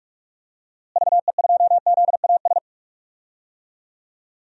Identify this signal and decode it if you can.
Morse code, heard as a keyed tone, about 46 wpm, 700 Hz. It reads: VE2ZAS
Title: DXCC-100 Call Signs - 6 Characters - Repeated 2x - 45wpm